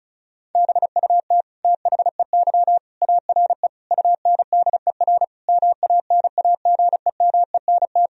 Morse code, heard as a keyed tone, about 36 wpm, 700 Hz. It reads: BUT THEY ARE UNDER MANAGEMENT
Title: Sentences from Top 1000 Words - Review Part 4 - 35wpm